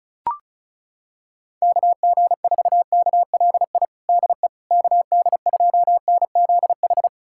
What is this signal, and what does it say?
Morse code, heard as a keyed tone, about 34 wpm, 700 Hz. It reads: KG4KLI DE KD2NZH
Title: USA Callsign DE Callsign 35wpm